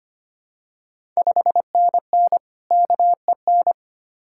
Morse code, heard as a keyed tone, about 25 wpm, 700 Hz. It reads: 5NN KEN
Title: Ontario QSO Party Exchange 25wpm